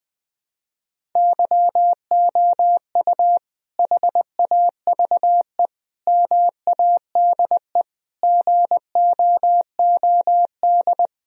Morse code, heard as a keyed tone, about 21 wpm, 700 Hz. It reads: YOU HAVE MADE GOOD
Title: Sentences from Top 1000 Words - Review Part 1 - 20wpm